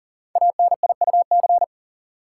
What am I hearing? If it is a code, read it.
Morse code, heard as a keyed tone, about 42 wpm, 700 Hz. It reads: ANIUC
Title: Code Groups - 5 Letter:Numbers - 40wpm